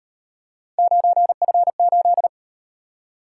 Morse code, heard as a keyed tone, about 37 wpm, 700 Hz. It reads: 9F8